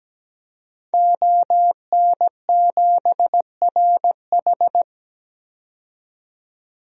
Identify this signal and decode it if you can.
Morse code, heard as a keyed tone, about 17 wpm, 700 Hz. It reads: ON7RH
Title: DXCC-100 Call Signs - 5 Characters - 17wpm